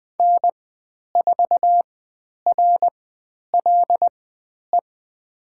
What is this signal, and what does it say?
Morse code, heard as a keyed tone, about 20 wpm, spaced as for 10 wpm, 700 Hz. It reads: N4RLE
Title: 109 Lesson 18g - Letter m - Call Signs 20wpm